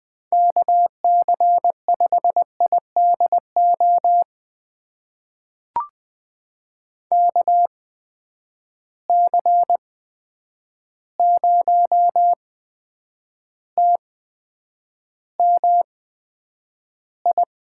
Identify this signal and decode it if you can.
Morse code, heard as a keyed tone, about 20 wpm, spaced as for 9 wpm, 700 Hz. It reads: KC5IDO KC0TMI
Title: USA Call Signs - 6 Characters - 8x Character Spacing - 20wpm